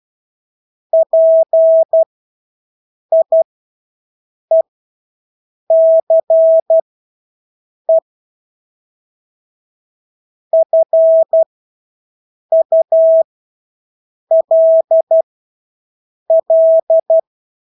Morse code, heard as a keyed tone, about 12 wpm, spaced as for 6 wpm, 650 Hz. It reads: PIECE FULL
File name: Engelske ord 12-6 wpm